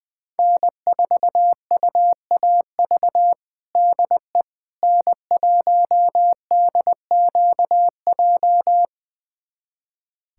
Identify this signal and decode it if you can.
Morse code, heard as a keyed tone, about 20 wpm, 700 Hz. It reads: N4UAV DE N1DQJ
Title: USA Callsign DE Callsign - 5 Characters - 20wpm